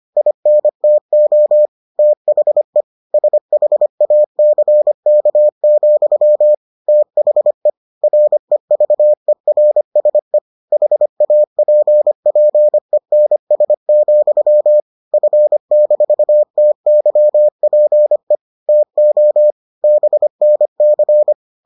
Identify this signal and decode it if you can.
Morse code, heard as a keyed tone, about 25 wpm, 600 Hz. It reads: INTO THE SHACK, THE REVERSE HAPPENS, F-TYPE TO BNC